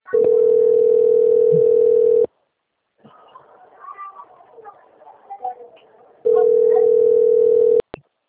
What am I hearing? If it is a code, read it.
Ringback tone (North America), 440 + 480 Hz, with speech.
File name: out- - - - - .50329